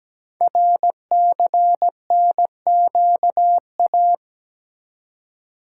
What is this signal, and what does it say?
Morse code, heard as a keyed tone, about 17 wpm, 700 Hz. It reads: RCNQA
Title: Code Groups - 5 Letters - 17wpm